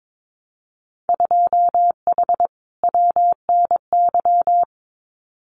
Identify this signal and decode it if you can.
Morse code, heard as a keyed tone, about 22 wpm, 700 Hz. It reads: 2H WNY
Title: Winter Field Day Exchanges Encore 22wpm